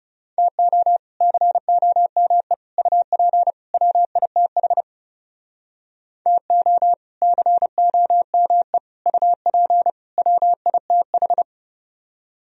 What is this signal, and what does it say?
Morse code, heard as a keyed tone, about 40 wpm, 700 Hz. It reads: TO COME UP WITH TO COME UP WITH